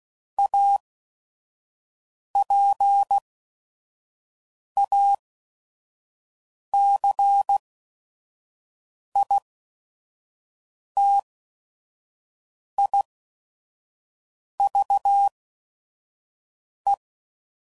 Morse code, heard as a keyed tone, about 16 wpm, 800 Hz. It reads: A P A C I T I V E